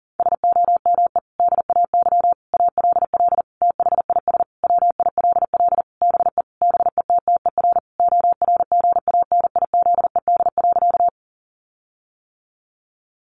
Morse code, heard as a keyed tone, about 37 wpm, 700 Hz. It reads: SOME DAY ALL THIS WILL BE BETTER ORGANIZED.
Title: Sentences from Top 1300 Words 40wpm